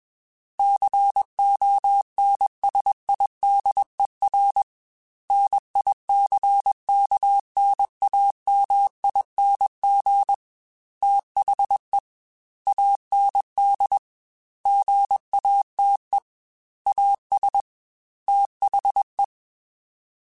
Morse code, heard as a keyed tone, about 21 wpm, 800 Hz. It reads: CONSIDER NICKNAMING THE AND GATE AS THE
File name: Technical Subjects / Technical Subjects-mp3s / AND Gate @20